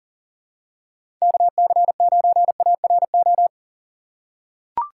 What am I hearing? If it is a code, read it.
Morse code, heard as a keyed tone, about 40 wpm, 700 Hz. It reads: KC9ARO
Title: USA Call Signs - 3x Character Spacing - 40wpm